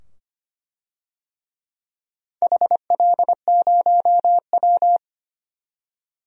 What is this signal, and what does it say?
Morse code, heard as a keyed tone, about 25 wpm, 700 Hz. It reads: HL0W
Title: DXCC-100 Call Signs - 4 Characters - 25wpm